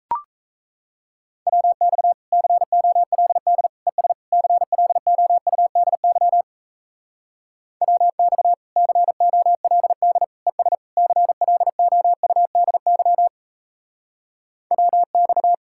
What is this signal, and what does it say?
Morse code, heard as a keyed tone, about 37 wpm, 700 Hz. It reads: WX COLD ES CLOUDY WX COLD ES CLOUDY WX